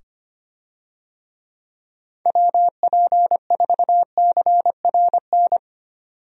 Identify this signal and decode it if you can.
Morse code, heard as a keyed tone, about 25 wpm, 700 Hz. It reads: WP4CRN
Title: USA Call Signs - 6 Characters - 25wpm